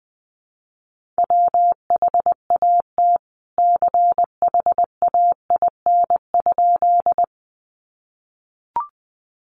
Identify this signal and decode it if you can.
Morse code, heard as a keyed tone, about 20 wpm, 700 Hz. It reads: WHAT CHAIN?
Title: Sentences from Top 1500 Words 20wpm